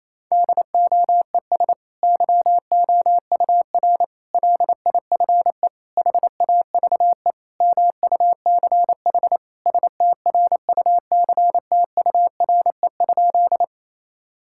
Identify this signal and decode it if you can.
Morse code, heard as a keyed tone, about 28 wpm, 700 Hz. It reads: DOES YOUR LIFE HAVE MUCH STRUCTURE?
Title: Sentences from Top 1000 Words - Review Part 6 - 28wpm